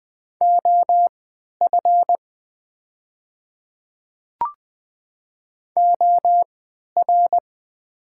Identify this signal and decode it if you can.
Morse code, heard as a keyed tone, about 20 wpm, spaced as for 11 wpm, 700 Hz. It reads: OF OR
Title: 3x Character Spacing - Top 500 Words - 20wpm